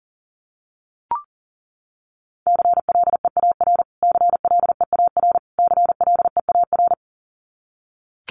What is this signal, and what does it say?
Morse code, heard as a keyed tone, about 37 wpm, 700 Hz. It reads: CLEAR CLEAR CLEAR
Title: Top 500 Words Repeated 40wpm